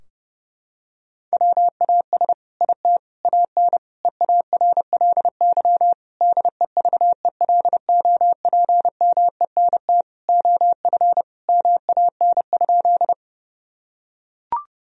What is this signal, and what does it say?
Morse code, heard as a keyed tone, about 30 wpm, 700 Hz. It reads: WAS IT AN EARLY DEVELOPMENT OF MAN?
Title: Sentences from Top 1000 Words - Review Part 4 - 30wpm